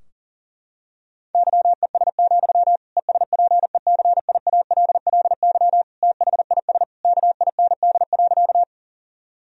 Morse code, heard as a keyed tone, about 40 wpm, 700 Hz. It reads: YES, ESPECIALLY THIS KIND.
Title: Sentences from Top 1000 Words - Review Part 4 - 40wpm